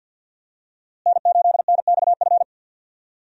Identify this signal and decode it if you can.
Morse code, heard as a keyed tone, about 49 wpm, 700 Hz. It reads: N8NXF
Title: USA Call Signs - 5 Characters - Repeated 2x - 50wpm